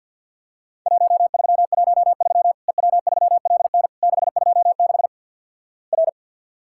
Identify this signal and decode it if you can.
Morse code, heard as a keyed tone, about 50 wpm, 700 Hz. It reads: 1313 EW3LN 626 R
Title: Worked All Europe DX Contest - QTC Exchange with Repeats - 50wpm